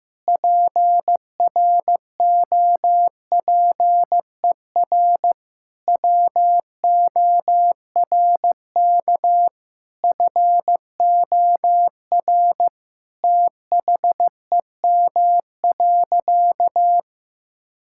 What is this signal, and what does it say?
Morse code, heard as a keyed tone, about 15 wpm, 700 Hz. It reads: PROPER WORK FOR THEM.
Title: Sentences from Top 1000 Words - Review Part 5 - 15wpm